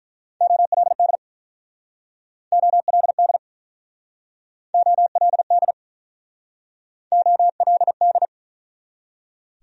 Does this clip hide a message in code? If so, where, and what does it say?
Yes: Morse code from 0.4 to 8.3 s, heard as a keyed tone, about 44 wpm, 700 Hz. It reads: OLD OLD OLD OLD